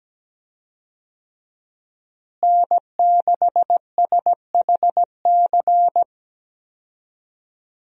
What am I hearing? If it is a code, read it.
Morse code, heard as a keyed tone, about 17 wpm, 700 Hz. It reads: N6SHC
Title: USA Call Signs - 5 Characters - Repeated 3x - 17wpm